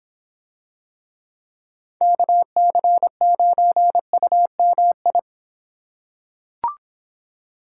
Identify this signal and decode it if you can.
Morse code, heard as a keyed tone, about 26 wpm, 700 Hz. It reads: KC9UMI